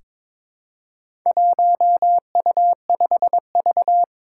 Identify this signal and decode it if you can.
Morse code, heard as a keyed tone, about 22 wpm, 700 Hz. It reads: 1U5V
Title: Four Challenging Letter-Number Combinations 22wpm